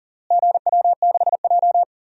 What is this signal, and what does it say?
Morse code, heard as a keyed tone, about 37 wpm, 700 Hz. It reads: GW6J
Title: DXCC-100 Call Signs - 4 Characters - Repeated 3x - 40wpm